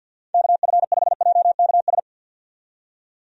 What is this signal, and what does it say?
Morse code, heard as a keyed tone, about 51 wpm, 700 Hz. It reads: KF5JKS
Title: USA Call Signs - 6 Characters - 50wpm